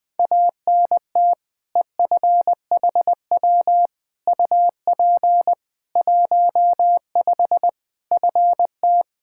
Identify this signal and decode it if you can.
Morse code, heard as a keyed tone, about 20 wpm, 700 Hz. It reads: ANT EFHW UP 15 FT
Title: Practice QSO - Exchange 2 - Info - 20wpm